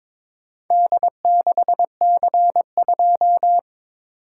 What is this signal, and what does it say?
Morse code, heard as a keyed tone, about 22 wpm, 700 Hz. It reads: D6C2
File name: practice / 22wpm / Four Letter-Number Combinations - 22wpm